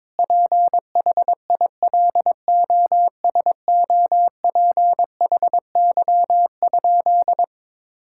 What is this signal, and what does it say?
Morse code, heard as a keyed tone, about 22 wpm, 700 Hz. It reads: PHILOSOPHY?
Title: Sentences from Top 1000 Words - Review Part 6 - 22wpm